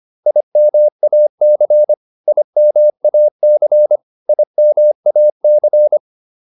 Morse code, heard as a keyed tone, about 25 wpm, 600 Hz. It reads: IMAC IMAC IMAC